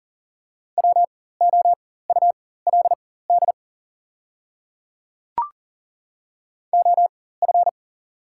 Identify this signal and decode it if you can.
Morse code, heard as a keyed tone, about 40 wpm, spaced as for 19 wpm, 690 Hz. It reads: WOULD OF
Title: 4x Character Spacing - Top 100 Words - 40wpm